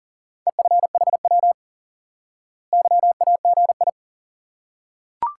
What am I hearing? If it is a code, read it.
Morse code, heard as a keyed tone, about 37 wpm, 700 Hz. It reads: EFHW YAGI
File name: IQE Course 3 - Lesson 89 - YAGI - Two Elements - 40wpm